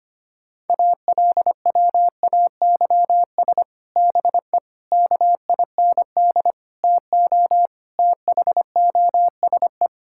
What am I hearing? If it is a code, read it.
Morse code, heard as a keyed tone, about 25 wpm, 700 Hz. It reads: ALWAYS BE KIND TO THOSE
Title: Sentences from Top 1600 Words 25wpm